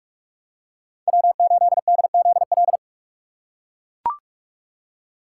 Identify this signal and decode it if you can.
Morse code, heard as a keyed tone, about 41 wpm, 700 Hz. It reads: W8DZL